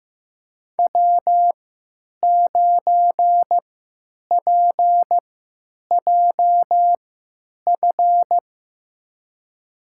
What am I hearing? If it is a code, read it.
Morse code, heard as a keyed tone, about 15 wpm, spaced as for 9 wpm, 700 Hz. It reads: W9PJF
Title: USA Call Signs - 5 Characters - 3x Character Spacing - 15wpm